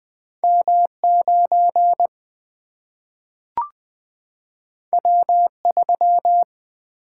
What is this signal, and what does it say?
Morse code, heard as a keyed tone, about 20 wpm, 700 Hz. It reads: M9 W3
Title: Single-Letter Single-Number 20wpm